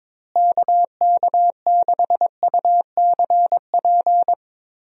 Morse code, heard as a keyed tone, about 22 wpm, 700 Hz. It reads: KK6UCP
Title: USA Call Signs - 6 Characters - Repeated 2x - 22wpm